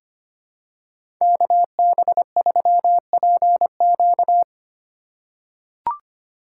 Morse code, heard as a keyed tone, about 25 wpm, 700 Hz. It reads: KB3PQ
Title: USA Call Signs - 5 Characters - 4x Character Spacing - 25wpm